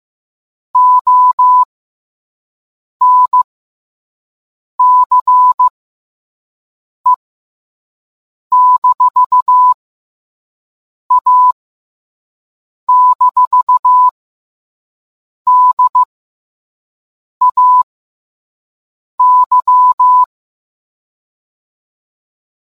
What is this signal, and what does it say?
Morse code, heard as a keyed tone, about 15 wpm, spaced as for 5 wpm, 1000 Hz. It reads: ONCE-A-DAY